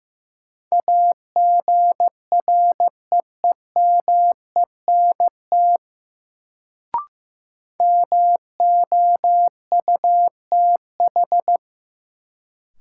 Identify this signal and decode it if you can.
Morse code, heard as a keyed tone, about 15 wpm, 700 Hz. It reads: AGREEMENT MOUTH
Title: Sentences from Top 1500 Words - New Word Drill - 15wpm